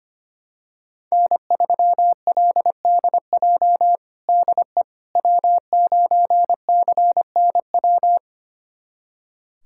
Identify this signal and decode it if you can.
Morse code, heard as a keyed tone, about 25 wpm, 700 Hz. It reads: N3LDJ DE W9CNW
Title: USA Callsign DE Callsign - 5 Characters - 25wpm